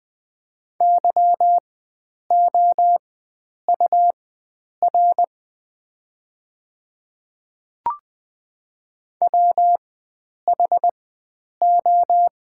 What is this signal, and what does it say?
Morse code, heard as a keyed tone, about 20 wpm, spaced as for 9 wpm, 690 Hz. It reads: YOUR WHO
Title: 4x Character Spacing - Top 100 Words - 20wpm